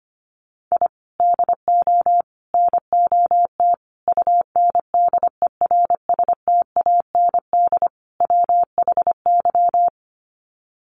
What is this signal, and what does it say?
Morse code, heard as a keyed tone, about 25 wpm, 700 Hz. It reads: I DO NOT UNDERSTAND WHY
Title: Sets of 5 Words Encore 25wpm